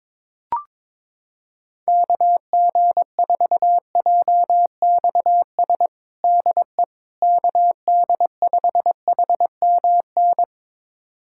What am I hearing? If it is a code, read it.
Morse code, heard as a keyed tone, about 22 wpm, 700 Hz. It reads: KG4JXS DE KD5HMN
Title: USA Callsign DE Callsign - 6 Characters - 22wpm